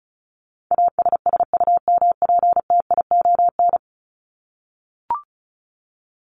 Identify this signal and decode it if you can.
Morse code, heard as a keyed tone, about 34 wpm, 700 Hz. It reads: ASSUMPTION
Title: Top 1751-2000 Words 35wpm